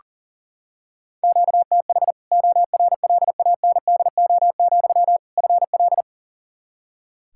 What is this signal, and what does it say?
Morse code, heard as a keyed tone, about 40 wpm, 700 Hz. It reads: QTH ORLANDO, FL
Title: Practice QSO - Exchange 1 - Beginning - 40wpm